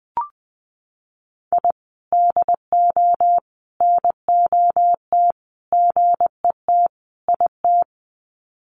Morse code, heard as a keyed tone, about 19 wpm, 700 Hz. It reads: I DO NOT GET IT
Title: Sets of 5 Words Encore 20wpm